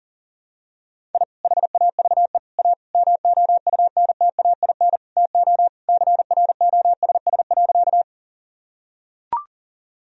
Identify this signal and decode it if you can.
Morse code, heard as a keyed tone, about 39 wpm, 700 Hz. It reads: I HAVE A MOUNTAIN TO CROSS.
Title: Sentences from Top 1000 Words - Review Part 1 - 40wpm